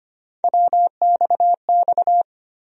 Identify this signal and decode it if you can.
Morse code, heard as a keyed tone, about 25 wpm, 700 Hz. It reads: WXX
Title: USA Call Sign Postfixes - 3 Letters 25wpm